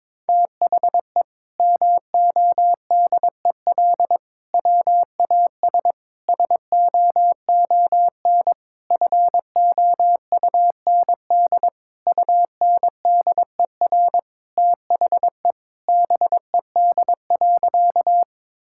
Morse code, heard as a keyed tone, about 22 wpm, 700 Hz. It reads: THE MODEL WAS SOON FOUND UNDER THE BED.
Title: Sentences from Top 1000 Words - Review Part 5 - 22wpm